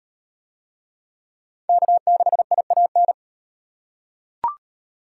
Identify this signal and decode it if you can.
Morse code, heard as a keyed tone, about 36 wpm, 700 Hz. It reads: K6IAN